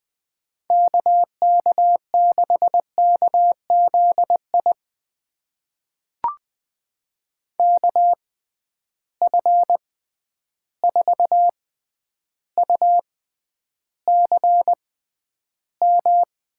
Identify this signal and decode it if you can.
Morse code, heard as a keyed tone, about 20 wpm, spaced as for 10 wpm, 700 Hz. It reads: KK6KZI KF4UCM
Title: USA Call Signs - 6x Character Spacing - 20wpm